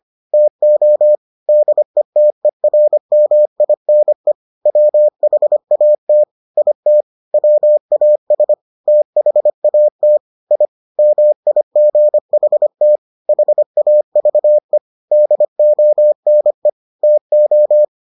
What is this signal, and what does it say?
Morse code, heard as a keyed tone, about 25 wpm, 600 Hz. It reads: TO DETERMINE WHAT IT WAS THAT I MIGHT HAVE DONE TO